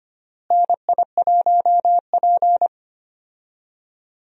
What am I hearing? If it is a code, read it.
Morse code, heard as a keyed tone, about 25 wpm, 700 Hz. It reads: NI1P
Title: USA Call Signs - 4 Characters - Repeated 3x - 25wpm